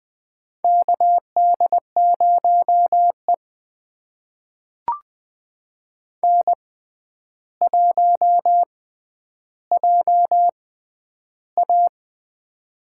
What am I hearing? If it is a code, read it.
Morse code, heard as a keyed tone, about 20 wpm, spaced as for 10 wpm, 700 Hz. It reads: KD0E N1JA
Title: USA Call Signs - 4 Characters - 6x Character Spacing - 20wpm